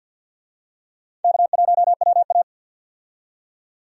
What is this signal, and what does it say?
Morse code, heard as a keyed tone, about 51 wpm, 700 Hz. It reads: K1WA
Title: USA Call Signs - 4 Characters - 50wpm